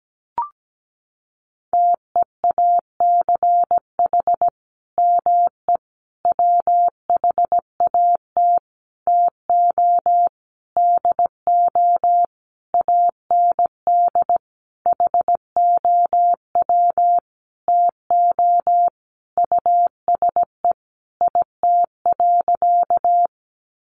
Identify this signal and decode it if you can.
Morse code, heard as a keyed tone, about 17 wpm, 700 Hz. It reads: TEACH ME WHAT TO DO AND HOW TO USE IT.
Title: Sentences from Top 500 Words 17wpm